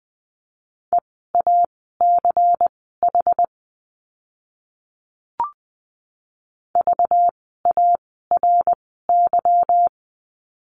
Morse code, heard as a keyed tone, about 20 wpm, spaced as for 14 wpm, 700 Hz. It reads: EACH VARY
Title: 192 Lesson 31f - Letter v - Words 20wpm